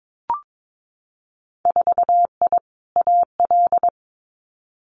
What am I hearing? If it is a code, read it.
Morse code, heard as a keyed tone, about 22 wpm, 700 Hz. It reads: 4I AL
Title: Winter Field Day Exchanges Encore 22wpm